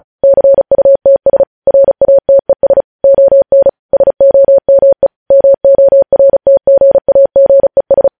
Morse code, heard as a keyed tone, about 36 wpm, 550 Hz. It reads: CUTS RATES ON SOME MORTGAGES
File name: podcast / News headlines-1x-35x35